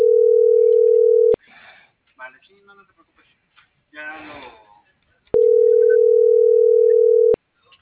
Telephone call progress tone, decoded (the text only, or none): ringback tone